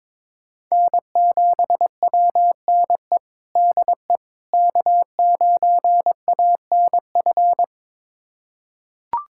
Morse code, heard as a keyed tone, about 22 wpm, 700 Hz. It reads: N7WNE DE K9ANF
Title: USA Callsign DE Callsign - 5 Characters - 22wpm